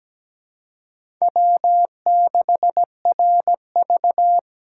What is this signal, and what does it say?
Morse code, heard as a keyed tone, about 17 wpm, 700 Hz. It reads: W6RV
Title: USA Call Signs - 4 Characters - 4x Character Spacing - 17wpm